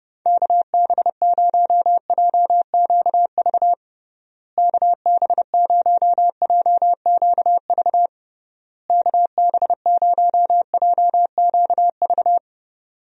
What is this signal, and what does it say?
Morse code, heard as a keyed tone, about 32 wpm, 700 Hz. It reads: KB0JQV KB0JQV KB0JQV
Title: USA Call Signs - 6 Characters - Repeated 3x - 30wpm